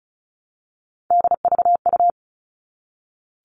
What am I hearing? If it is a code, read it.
Morse code, heard as a keyed tone, about 34 wpm, 700 Hz. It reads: DVU
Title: All Cut Numbers - 3-Digit 35wpm